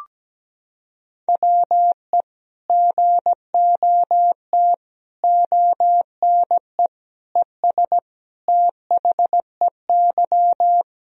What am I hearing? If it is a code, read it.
Morse code, heard as a keyed tone, about 17 wpm, 700 Hz. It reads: WE GOT ONE ES THEY
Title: Sentences from Top 300 Words and Abbreviations 17wpm